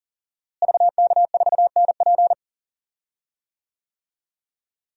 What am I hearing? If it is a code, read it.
Morse code, heard as a keyed tone, about 40 wpm, 700 Hz. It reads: VK4NP
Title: DXCC-100 Call Signs - 5 Characters - Repeated 3x - 40wpm